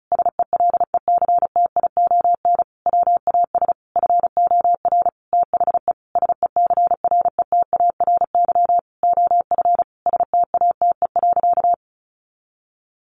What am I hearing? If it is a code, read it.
Morse code, heard as a keyed tone, about 34 wpm, 700 Hz. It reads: SELECTION WAS FOR THE SECRETARY OF STATE.
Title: Sentences from Top 1400 Words 35wpm